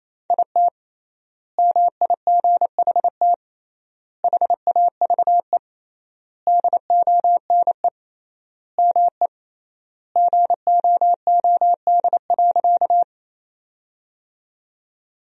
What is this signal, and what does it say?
Morse code, heard as a keyed tone, about 28 wpm, 700 Hz. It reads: IT MIGHT HAVE DONE ME GOOD.
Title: 3x Word-Spacing - Sentences from Top 500 Words - 28wpm